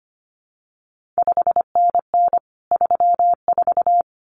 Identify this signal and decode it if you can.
Morse code, heard as a keyed tone, about 25 wpm, 700 Hz. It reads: 5NN 34
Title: CQ WW DX Contest Exchange Encore - 25wpm